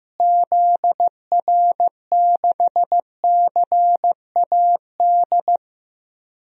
Morse code, heard as a keyed tone, about 15 wpm, 700 Hz. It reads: ZR6CAD
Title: DXCC-100 Call Signs - 6 Characters - Repeated 2x - 15wpm